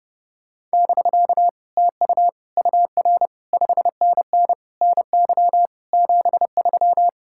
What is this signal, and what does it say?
Morse code, heard as a keyed tone, about 33 wpm, 700 Hz. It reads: <BK> TU UR 5NN NY 73